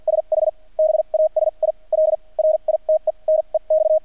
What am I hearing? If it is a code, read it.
Morse code, heard as a keyed tone, about 50 wpm, spaced as for 38 wpm, 630 Hz. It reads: SH BASIL RITE NEX
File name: archived / CRSnet